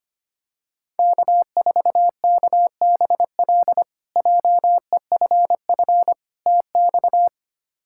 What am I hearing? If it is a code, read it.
Morse code, heard as a keyed tone, about 25 wpm, 700 Hz. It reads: K4KBL JEFF TX
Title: K1USN SST QSO Practice - 25wpm